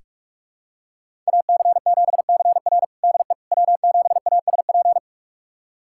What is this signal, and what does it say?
Morse code, heard as a keyed tone, about 42 wpm, 700 Hz. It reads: AC7CR DE W7ASP
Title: USA Callsign DE Callsign - 5 Characters - 45wpm